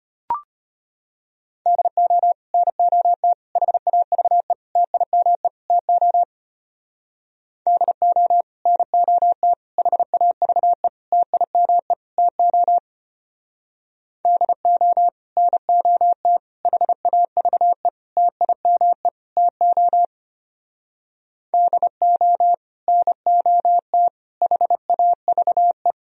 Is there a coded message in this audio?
Yes: Morse code from 1.7 to 25.9 s, heard as a keyed tone, about 33 wpm, 700 Hz. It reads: DO NOT HAVE TIME TO DO NOT HAVE TIME TO DO NOT HAVE TIME TO DO NOT HAVE